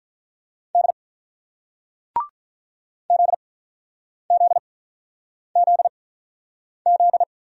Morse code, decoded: D ZZZZ

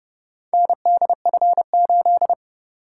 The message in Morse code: NDF8

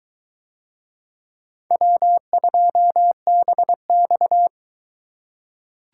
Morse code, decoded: W2BX

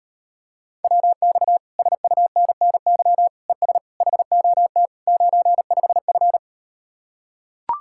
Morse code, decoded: WX SUNNY ES HOT 95F

